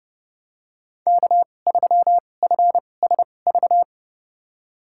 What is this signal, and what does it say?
Morse code, heard as a keyed tone, about 30 wpm, spaced as for 22 wpm, 700 Hz. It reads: K3FSV
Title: USA Call Signs - 5 Characters - 2x Character Spacing - 30wpm